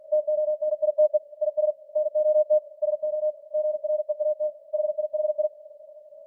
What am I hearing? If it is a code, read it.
Morse code, heard as a keyed tone, about 45 wpm, 610 Hz. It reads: TONITE IS NOT SO GREAT HIHI